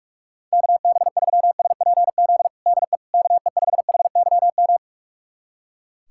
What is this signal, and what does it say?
Morse code, heard as a keyed tone, about 42 wpm, 690 Hz. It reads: KB3SPZ DE KE5HYM
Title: USA Callsign DE Callsign - 6 Characters - 45wpm